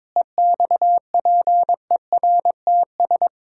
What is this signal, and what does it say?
Morse code, heard as a keyed tone, about 22 wpm, 700 Hz. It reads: EXPERTS
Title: Pangram Challenge v2 - 22wpm